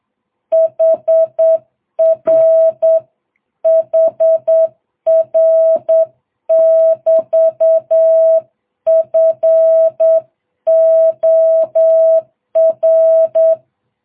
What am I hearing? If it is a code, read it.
Morse code, heard as a keyed tone, about 8 wpm, 660 Hz. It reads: HR HR <BT> FOR